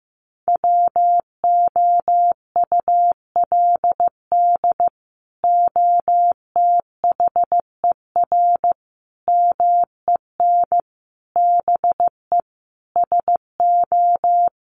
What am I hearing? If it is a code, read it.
Morse code, heard as a keyed tone, about 15 wpm, 700 Hz. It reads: WOULD OTHER MEN BE SO
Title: Sentences from Top 1400 Words 15wpm